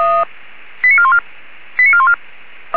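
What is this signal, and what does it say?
Five-tone selective call (ZVEI), heard as quick runs of five tones, tones of about 70 ms. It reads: 89414 89414
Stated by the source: ZVEI five-tone selective call received with an SDR.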